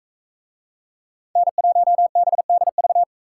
Morse code, heard as a keyed tone, about 40 wpm, 700 Hz. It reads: N1BDV